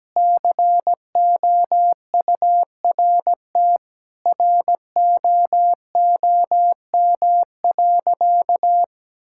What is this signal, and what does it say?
Morse code, heard as a keyed tone, about 17 wpm, 700 Hz. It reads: COURT ROOM.
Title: Sentences from Top 1600 Words 17wpm